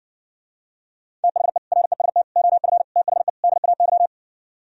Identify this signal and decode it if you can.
Morse code, heard as a keyed tone, about 69 wpm, 700 Hz. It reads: THE REST OF THE DAY